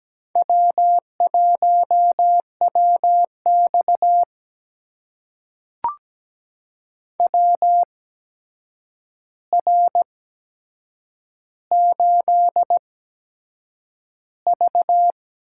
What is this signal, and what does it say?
Morse code, heard as a keyed tone, about 17 wpm, spaced as for 7 wpm, 700 Hz. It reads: W1WX WR8V
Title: USA Call Signs - 4 Characters - 8x Character Spacing - 17wpm